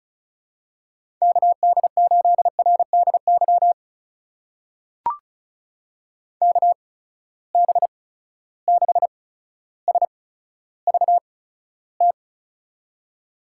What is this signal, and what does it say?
Morse code, heard as a keyed tone, about 34 wpm, spaced as for 15 wpm, 700 Hz. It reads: KD8RDY KB6SVT
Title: USA Call Signs - 6 Characters - 8x Character Spacing - 35wpm